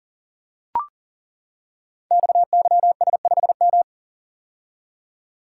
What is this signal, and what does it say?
Morse code, heard as a keyed tone, about 41 wpm, 700 Hz. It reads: XYS5M
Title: Code Groups - 5 Letter:Numbers - 40wpm